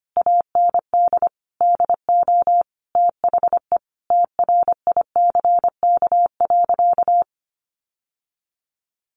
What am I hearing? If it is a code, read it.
Morse code, heard as a keyed tone, about 25 wpm, 700 Hz. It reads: AND DO THE TRICK.
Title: Sentences from Top 1500 Words 25wpm